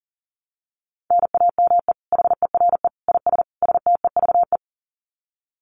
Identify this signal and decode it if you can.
Morse code, heard as a keyed tone, about 37 wpm, 700 Hz. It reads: NAME HERE IS STEVE
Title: Common QSO Phrases 40wpm